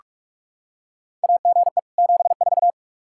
Morse code, heard as a keyed tone, about 45 wpm, 700 Hz. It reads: AGE 74
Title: Practice QSO - Exchange 3 - More Info - 45wpm